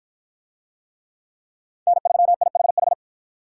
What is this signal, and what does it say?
Morse code, heard as a keyed tone, about 53 wpm, 700 Hz. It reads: N3IHH